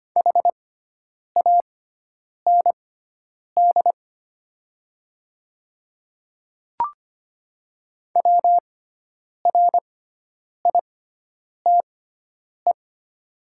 Morse code, heard as a keyed tone, about 25 wpm, spaced as for 9 wpm, 700 Hz. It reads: HAND WRITE
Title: 6x Character Spacing - Top 300 Words - 25wpm